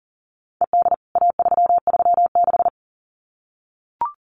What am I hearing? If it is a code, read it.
Morse code, heard as a keyed tone, about 40 wpm, 700 Hz. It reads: ED A336
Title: CWT Exchange 40wpm